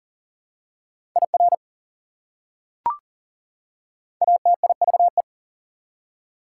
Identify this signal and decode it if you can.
Morse code, heard as a keyed tone, about 40 wpm, 700 Hz. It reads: IR ATIVE
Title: Prefixes and Suffixes 40wpm